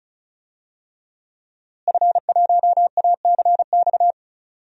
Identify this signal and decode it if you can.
Morse code, heard as a keyed tone, about 36 wpm, 700 Hz. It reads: F1ACX
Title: DXCC-100 Call Signs - 5 Characters - Repeated 3x - 35wpm